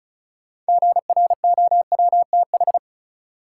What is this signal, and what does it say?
Morse code, heard as a keyed tone, about 34 wpm, 700 Hz. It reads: GROWTH
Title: Sentences from Top 1000 Words - Review Part 5 - New Word Drill - 35wpm